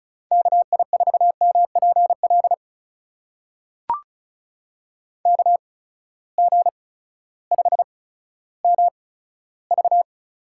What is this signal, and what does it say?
Morse code, heard as a keyed tone, about 36 wpm, spaced as for 16 wpm, 700 Hz. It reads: KI4MPL KG5MV
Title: USA Call Signs - 6 Characters - 8x Character Spacing - 35wpm